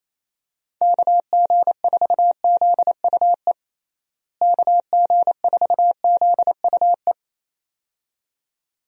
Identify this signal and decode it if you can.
Morse code, heard as a keyed tone, about 27 wpm, 700 Hz. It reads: KG4ZUE KG4ZUE
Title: USA Call Signs - 6 Characters - Repeated 2x - 28wpm